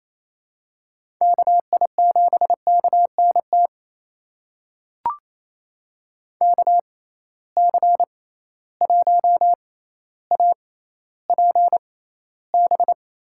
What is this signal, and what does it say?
Morse code, heard as a keyed tone, about 28 wpm, spaced as for 14 wpm, 700 Hz. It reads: KI7KNT KC1APB
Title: USA Call Signs - 6 Characters - 6x Character Spacing - 28wpm